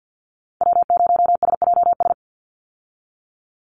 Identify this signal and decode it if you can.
Morse code, heard as a keyed tone, about 50 wpm, 700 Hz. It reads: W0SJS
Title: Call Signs Repeated 2x - 50wpm